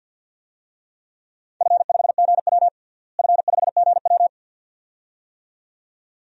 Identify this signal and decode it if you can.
Morse code, heard as a keyed tone, about 52 wpm, 700 Hz. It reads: F5GW F5GW
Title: DXCC-100 Call Signs - 4 Characters - Repeated 2x - 50wpm